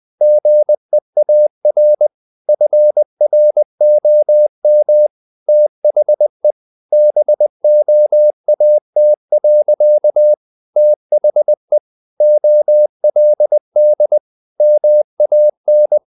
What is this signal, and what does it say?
Morse code, heard as a keyed tone, about 20 wpm, 600 Hz. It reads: GEAR FROM THE BOAT. THE OLD MAN
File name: the old man and the sea / the old man and the sea part 1 20wpm